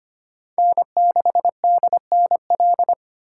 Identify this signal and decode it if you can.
Morse code, heard as a keyed tone, about 32 wpm, 700 Hz. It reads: N6DNL